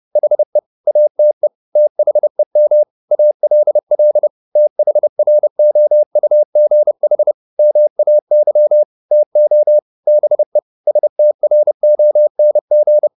Morse code, heard as a keyed tone, about 30 wpm, 600 Hz. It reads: HE ATE THEM ALL THROUGH MAY TO BE STRONG
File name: the old man and the sea / the old man and the sea 1 30wpm